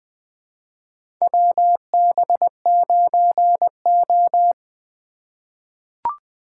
Morse code, heard as a keyed tone, about 20 wpm, 700 Hz. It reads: WB9O
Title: USA Call Signs - 4 Characters - 2x Character Spacing - 20wpm